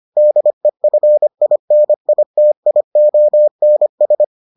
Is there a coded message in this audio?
Yes: Morse code from 0.2 to 4.2 s, heard as a keyed tone, about 25 wpm, 600 Hz. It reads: DEFINITIONS